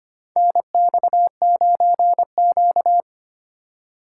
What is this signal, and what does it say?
Morse code, heard as a keyed tone, about 25 wpm, 700 Hz. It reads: NX9Q
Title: USA Call Signs - 4x Character Spacing - 25wpm